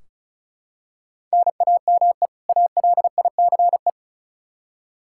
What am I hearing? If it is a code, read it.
Morse code, heard as a keyed tone, about 36 wpm, 700 Hz. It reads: NAME ALICE
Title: Practice QSO - Exchange 1 - Beginning - 35wpm